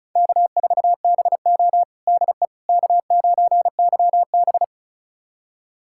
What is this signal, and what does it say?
Morse code, heard as a keyed tone, about 36 wpm, 700 Hz. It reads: K4BO DE K9YB
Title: USA Callsign DE Callsign - 4 Characters - 35wpm